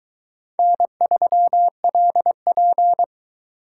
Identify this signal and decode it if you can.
Morse code, heard as a keyed tone, about 23 wpm, 700 Hz. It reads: N3LP